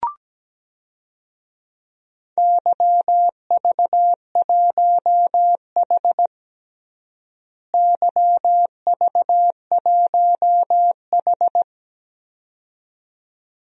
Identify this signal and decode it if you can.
Morse code, heard as a keyed tone, about 17 wpm, 700 Hz. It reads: YV1H YV1H
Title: DXCC-100 Call Signs - 4 Characters - Repeated 2x - 17wpm